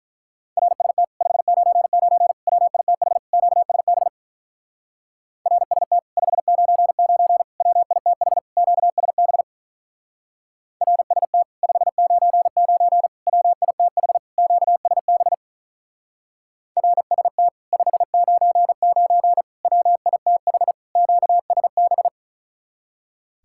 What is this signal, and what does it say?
Morse code, heard as a keyed tone, about 44 wpm, 700 Hz. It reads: RST 599 WITH QSB RST 599 WITH QSB RST 599 WITH QSB RST 599 WITH QSB